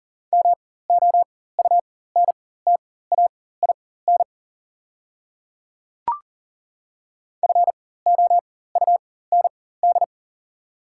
Morse code, heard as a keyed tone, about 40 wpm, spaced as for 19 wpm, 690 Hz. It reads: MOUNTAIN FOUND
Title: 4x Character Spacing - Top 500 Words - 40wpm